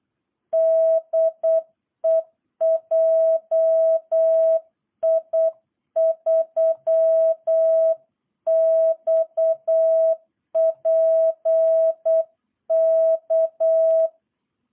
Morse code, heard as a keyed tone, about 8 wpm, 660 Hz. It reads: DE JI3XPK
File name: jh8cbh / 2022cw